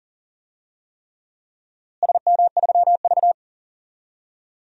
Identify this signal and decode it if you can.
Morse code, heard as a keyed tone, about 40 wpm, 700 Hz. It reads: SM3V
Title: DXCC-100 Call Signs - 4 Characters - Repeated 3x - 40wpm